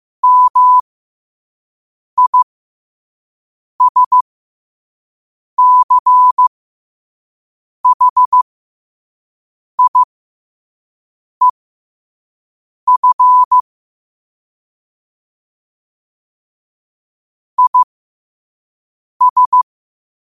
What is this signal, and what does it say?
Morse code, heard as a keyed tone, about 15 wpm, spaced as for 5 wpm, 1000 Hz. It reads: MISCHIEF IS
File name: qotdfiles / 5wpm